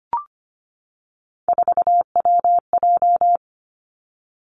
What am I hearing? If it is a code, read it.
Morse code, heard as a keyed tone, about 25 wpm, 700 Hz. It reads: 4WJ
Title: Three Challenging Letter-Number Combinations 25wpm